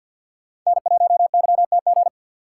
Morse code, heard as a keyed tone, about 46 wpm, 700 Hz. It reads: N1YTG
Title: USA Call Signs - 5 Characters - 2x Character Spacing - 50wpm